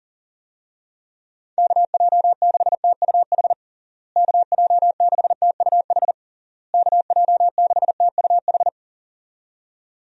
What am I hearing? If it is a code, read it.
Morse code, heard as a keyed tone, about 40 wpm, 700 Hz. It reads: KJ6TUH KJ6TUH KJ6TUH
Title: USA Call Signs - 6 Characters - Repeated 3x - 40wpm